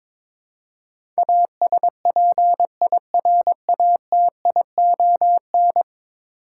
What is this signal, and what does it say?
Morse code, heard as a keyed tone, about 21 wpm, 700 Hz. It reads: ASPIRATION
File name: practice / 22wpm / 10 Letter Words - First 3 Letters Given - 22wpm